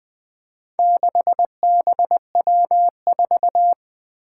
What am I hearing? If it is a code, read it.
Morse code, heard as a keyed tone, about 20 wpm, 700 Hz. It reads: 6BW4
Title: Four Challenging Letter-Number Combinations 20wpm